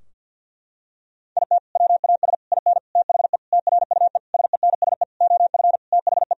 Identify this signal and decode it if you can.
Morse code, heard as a keyed tone, about 50 wpm, 700 Hz. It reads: IT WAS IN THE TRUE SENSE OF THE
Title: Sentences from Top 1000 Words - Review Part 4 - 50wpm